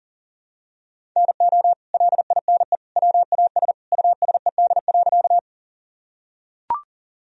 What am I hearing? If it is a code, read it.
Morse code, heard as a keyed tone, about 40 wpm, 700 Hz. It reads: NO LINE WAS USED.
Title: Sentences from Top 1000 Words - Review Part 1 - 40wpm